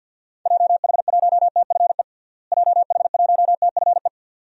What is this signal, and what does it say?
Morse code, heard as a keyed tone, about 53 wpm, 700 Hz. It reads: JH1TFE JH1TFE
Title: DXCC-100 Call Signs - 6 Characters - Repeated 2x - 50wpm